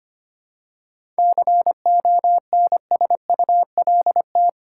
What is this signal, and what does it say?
Morse code, heard as a keyed tone, about 25 wpm, 700 Hz. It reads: CONSULT
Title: Seven Letter Words - 25wpm